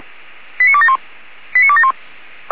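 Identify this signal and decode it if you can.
Five-tone selective call (ZVEI), heard as quick runs of five tones, tones of about 70 ms. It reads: 89271 89271
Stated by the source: ZVEI five-tone call 89273 received by SDR.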